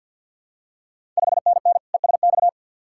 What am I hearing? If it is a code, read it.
Morse code, heard as a keyed tone, about 51 wpm, 690 Hz. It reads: 5NN ESX
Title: Ontario QSO Party Exchange 50wpm